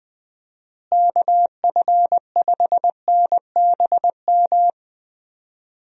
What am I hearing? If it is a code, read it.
Morse code, heard as a keyed tone, about 20 wpm, 700 Hz. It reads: KF5NBM
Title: USA Call Signs - 6 Characters - Repeated 2x - 20wpm